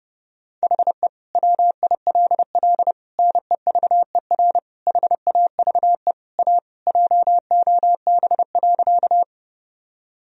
Morse code, heard as a keyed tone, about 30 wpm, 700 Hz. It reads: HE WILL NEVER HAVE A JOB.
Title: Sentences from Top 1000 Words - Review Part 3 - 30wpm